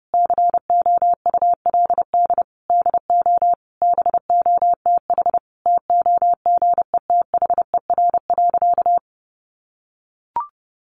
Morse code, heard as a keyed tone, about 30 wpm, 700 Hz. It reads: COULD DO BOTH TOGETHER.
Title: Sentences from Top 1300 Words 30wpm